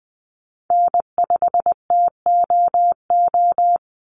Morse code, heard as a keyed tone, about 20 wpm, 700 Hz. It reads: N5TOO
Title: Call Signs - 20wpm